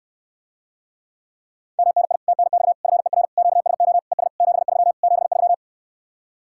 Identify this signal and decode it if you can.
Morse code, heard as a keyed tone, about 69 wpm, 700 Hz. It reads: MNI TNX FER QSO ES 73 73